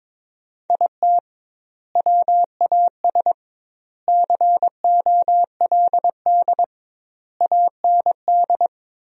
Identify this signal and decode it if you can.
Morse code, heard as a keyed tone, about 22 wpm, 700 Hz. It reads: IT WAS COLD AND
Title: 2x Word-Spacing - Sentences from Top 500 Words - 22wpm